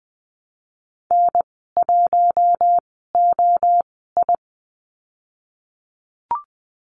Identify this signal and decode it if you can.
Morse code, heard as a keyed tone, about 20 wpm, spaced as for 14 wpm, 700 Hz. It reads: N1OI
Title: 048 Lesson 9g - Number 4 - Call Signs 20wpm